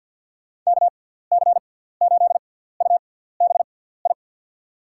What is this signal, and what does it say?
Morse code, heard as a keyed tone, about 53 wpm, spaced as for 17 wpm, 700 Hz. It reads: KC8UBI
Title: USA Call Signs - 6 Characters - 6x Character Spacing - 50wpm